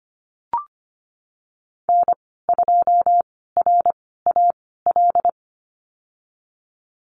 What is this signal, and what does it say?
Morse code, heard as a keyed tone, about 25 wpm, spaced as for 16 wpm, 700 Hz. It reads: N2RAL
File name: Lesson 15g - Number 5 - Call Signs 25wpm